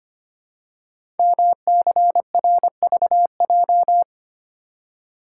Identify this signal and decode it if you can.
Morse code, heard as a keyed tone, about 25 wpm, 700 Hz. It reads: MCRVJ
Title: Code Groups - 5 Letters - 25wpm